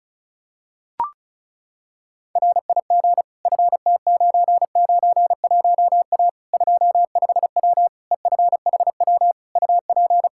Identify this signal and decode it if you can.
Morse code, heard as a keyed tone, about 36 wpm, 700 Hz. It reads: RIG FT991A 25W EFHW UP
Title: Practice QSO - Exchange 2 - Info - 35wpm